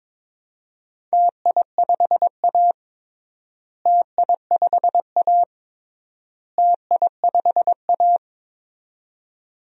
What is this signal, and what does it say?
Morse code, heard as a keyed tone, about 22 wpm, 700 Hz. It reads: TI5A TI5A TI5A
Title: DXCC-100 Call Signs - 4 Characters - Repeated 3x - 22wpm